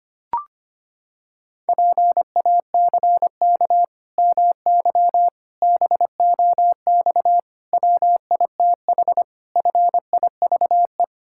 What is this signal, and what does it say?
Morse code, heard as a keyed tone, about 25 wpm, 700 Hz. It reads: PACK MY BOX WITH FIVE
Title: Pangram Challenge v2 - 25wpm